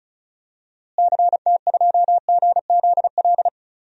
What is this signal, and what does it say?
Morse code, heard as a keyed tone, about 34 wpm, 700 Hz. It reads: CT2GZL
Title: DXCC-100 Call Signs - 6 Characters - Repeated 2x - 35wpm